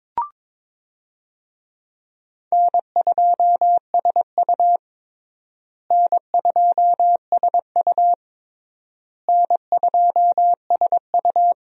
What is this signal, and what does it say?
Morse code, heard as a keyed tone, about 22 wpm, 700 Hz. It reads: N2SU N2SU N2SU
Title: USA Call Signs - 4 Characters - Repeated 3x - 22wpm